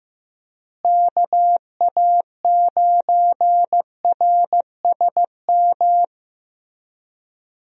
Morse code, heard as a keyed tone, about 15 wpm, 700 Hz. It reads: KA9RSM
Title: USA Call Signs - 6 Characters - Repeated 3x - 15wpm